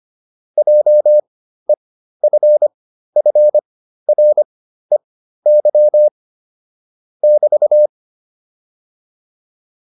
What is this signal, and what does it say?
Morse code, heard as a keyed tone, about 25 wpm, spaced as for 13 wpm, 600 Hz. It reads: JEFFREY <BT>